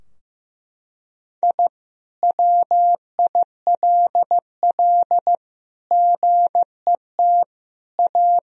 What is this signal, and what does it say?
Morse code, heard as a keyed tone, about 15 wpm, 700 Hz. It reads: I WILL GET A
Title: Sentences from Top 1000 Words - Review Part 2 - 15wpm